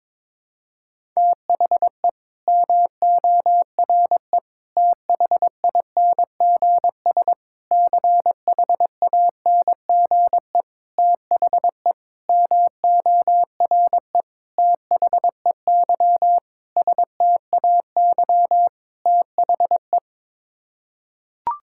Morse code, heard as a keyed tone, about 22 wpm, 700 Hz. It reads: THE MORE THINGS CHANGE THE MORE THEY STAY THE
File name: Sets of 9 Words - 22wpm